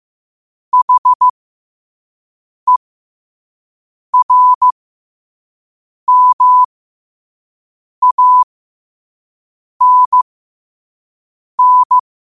Morse code, heard as a keyed tone, about 15 wpm, spaced as for 5 wpm, 1000 Hz. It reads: HERMANN
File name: qotdfiles / 5wpm